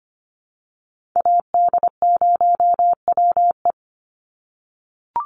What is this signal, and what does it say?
Morse code, heard as a keyed tone, about 25 wpm, 700 Hz. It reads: AD0WE
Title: Call Signs - 25wpm